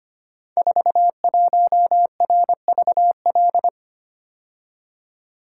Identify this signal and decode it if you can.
Morse code, heard as a keyed tone, about 25 wpm, 700 Hz. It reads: 41RVL
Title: Code Groups - 5 Letter:Numbers - 25wpm